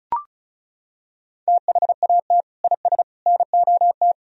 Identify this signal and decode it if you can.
Morse code, heard as a keyed tone, about 36 wpm, 700 Hz. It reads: THAT IS NOT
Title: Sentences from Top 1000 Words - Review Part 4 - 35wpm